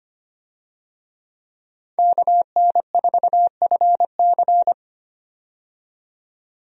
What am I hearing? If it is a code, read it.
Morse code, heard as a keyed tone, about 25 wpm, 700 Hz. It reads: KN4FC
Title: USA Call Signs - 5 Characters - 25wpm